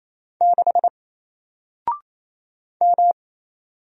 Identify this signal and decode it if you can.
Morse code, heard as a keyed tone, about 27 wpm, 700 Hz. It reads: 6 M
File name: Single-Letter-Number-28wpm